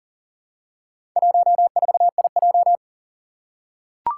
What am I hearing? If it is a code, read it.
Morse code, heard as a keyed tone, about 40 wpm, 700 Hz. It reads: 14IJ
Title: Four Challenging Letter-Number Combinations 40wpm